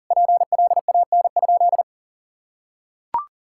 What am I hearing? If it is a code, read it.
Morse code, heard as a keyed tone, about 40 wpm, 700 Hz. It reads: PLAN?
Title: Sentences from Top 1000 Words - Review Part 6 - 40wpm